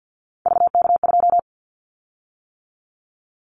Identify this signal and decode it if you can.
Morse code, heard as a keyed tone, about 50 wpm, 700 Hz. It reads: VK2
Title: Callsign Prefixes for Top 50 DXCC - 50wpm